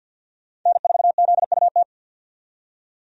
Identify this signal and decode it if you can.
Morse code, heard as a keyed tone, about 50 wpm, 700 Hz. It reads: N4ZUT
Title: USA Call Signs - 5 Characters - 50wpm